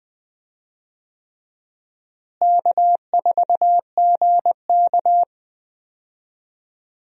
Morse code, heard as a keyed tone, about 20 wpm, 700 Hz. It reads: K4GK
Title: USA Call Signs - 4 Characters - Repeated 3x - 20wpm